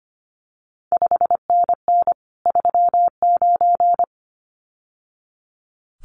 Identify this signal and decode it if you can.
Morse code, heard as a keyed tone, about 25 wpm, 700 Hz. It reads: 5NN 39
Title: CQ WW DX Contest Exchange - 25wpm